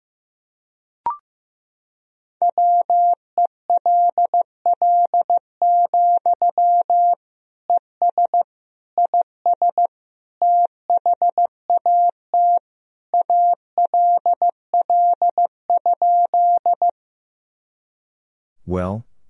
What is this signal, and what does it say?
Morse code, heard as a keyed tone, about 16 wpm, 700 Hz. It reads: WELL, ES IS THAT ALL?
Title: Sentences from Top 200 Words and Abbreviations 15wpm